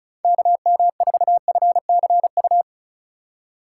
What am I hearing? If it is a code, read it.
Morse code, heard as a keyed tone, about 36 wpm, 700 Hz. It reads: KM4FCU
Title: USA Call Signs - 4x Character Spacing - 35wpm